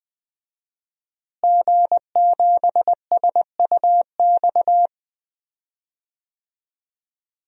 Morse code, heard as a keyed tone, about 20 wpm, 700 Hz. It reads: G7SUX
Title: DXCC-100 Call Signs - 5 Characters - Repeated 2x - 20wpm